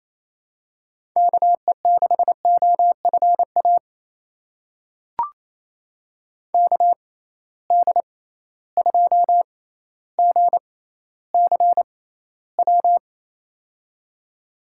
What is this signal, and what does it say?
Morse code, heard as a keyed tone, about 28 wpm, spaced as for 14 wpm, 700 Hz. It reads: KE6OFA KD2GCW
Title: USA Call Signs - 6 Characters - 6x Character Spacing - 28wpm